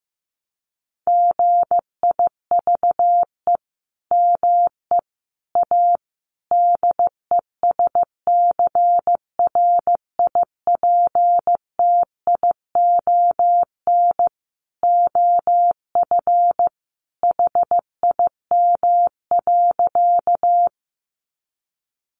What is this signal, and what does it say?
Morse code, heard as a keyed tone, about 15 wpm, 700 Hz. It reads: GIVE ME A DESCRIPTION OF HIM.
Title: Sentences from Top 1300 Words 15wpm